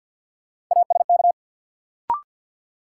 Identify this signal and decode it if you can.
Morse code, heard as a keyed tone, about 49 wpm, 700 Hz. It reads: ASK
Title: Sentences from Top 200 Words - New Word Drill - 50wpm